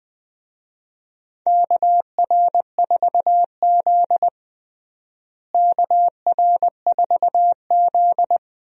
Morse code, heard as a keyed tone, about 20 wpm, 700 Hz. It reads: KR4Z KR4Z
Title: USA Call Signs - 4 Characters - Repeated 3x - 20wpm